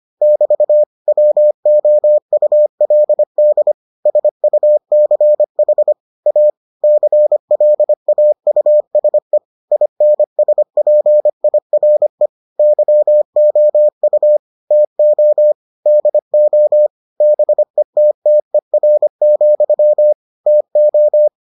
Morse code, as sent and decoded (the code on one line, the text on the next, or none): -...- / .-- --- ..- .-.. -.. / ... ..- -.-. .... / .- / -.-. .-.. .- ..- ... / .. -. ... .--. .. .-. / -.-- --- ..- / - --- / -.. --- / -... - - . .-. --..-- / - ---
<BT> WOULD SUCH A CLAUSE INSPIRE YOU TO DO BETTER, TO